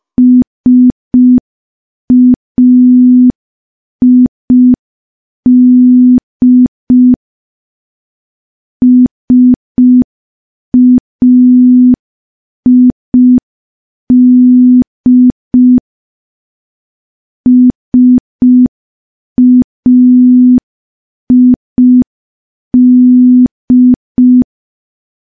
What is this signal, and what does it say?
Morse code, heard as a keyed tone, about 5 wpm, 250 Hz. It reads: SAID SAID SAID
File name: morse / morse-training-05WPM